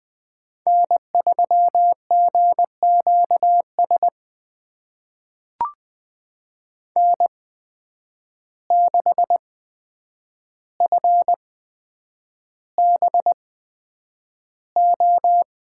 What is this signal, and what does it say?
Morse code, heard as a keyed tone, about 20 wpm, spaced as for 9 wpm, 700 Hz. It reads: N3GQS N6FBO
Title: USA Call Signs - 5 Characters - 8x Character Spacing - 20wpm